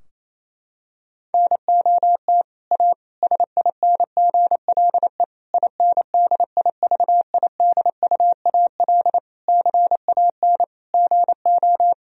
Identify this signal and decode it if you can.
Morse code, heard as a keyed tone, about 28 wpm, 700 Hz. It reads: NOT A SINGLE INDIVIDUAL CAN GO
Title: Sentences from Top 1000 Words - Review Part 4 - 28wpm